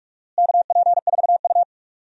Morse code, heard as a keyed tone, about 43 wpm, 700 Hz. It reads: KP4U
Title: DXCC-100 Call Signs - 4 Characters - Repeated 2x - 45wpm